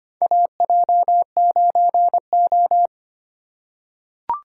Morse code, heard as a keyed tone, about 25 wpm, 700 Hz. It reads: AJ9O
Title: USA Call Signs - 4 Characters - 2x Character Spacing - 25wpm